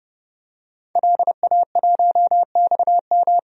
Morse code, heard as a keyed tone, about 30 wpm, 700 Hz. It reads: LA1XM
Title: DXCC-100 Call Signs - 5 Characters - 30wpm